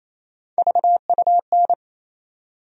Morse code, heard as a keyed tone, about 27 wpm, 700 Hz. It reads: VUN
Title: USA Call Sign Postfixes - 3 Letters 28wpm